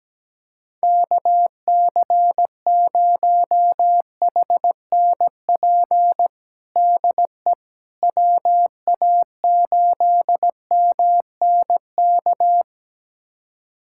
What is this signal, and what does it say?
Morse code, heard as a keyed tone, about 17 wpm, 700 Hz. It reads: KC0HNP DE WA8MNK
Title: USA Callsign DE Callsign 17wpm